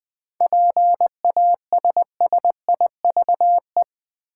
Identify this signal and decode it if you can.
Morse code, heard as a keyed tone, about 20 wpm, 700 Hz. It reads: PASSIVE
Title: 7 Letter Words - First 3 Letters Given - 20wpm